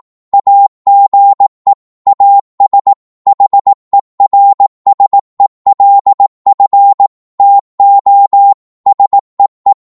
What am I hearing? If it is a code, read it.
Morse code, heard as a keyed tone, about 18 wpm, 800 Hz. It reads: AGE AS HERSELF TO SEE
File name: Morse / psg-18wpm